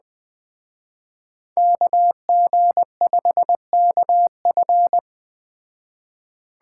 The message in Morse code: KG5KF

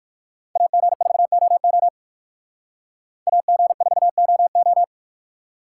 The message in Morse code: AG4OO AG4OO

